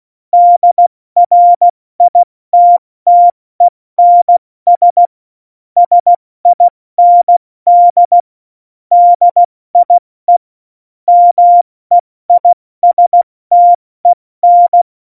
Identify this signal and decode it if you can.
Morse code, heard as a keyed tone, about 16 wpm, spaced as for 14 wpm, 700 Hz. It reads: DRITTENS SIND DIE MEISTEN
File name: Morsekurs / Witze / Witz 26. Januar 2021, 70 F80